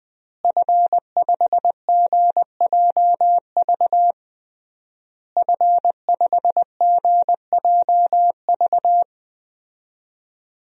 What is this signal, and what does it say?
Morse code, heard as a keyed tone, about 21 wpm, 700 Hz. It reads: F5GJV F5GJV
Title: DXCC-100 Call Signs - 5 Characters - Repeated 3x - 20wpm